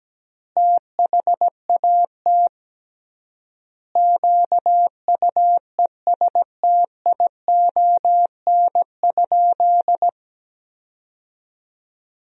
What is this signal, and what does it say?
Morse code, heard as a keyed tone, about 17 wpm, 700 Hz. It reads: THAT QUESTION?
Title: 3x Word-Spacing - Sentences from Top 400 Words - 17wpm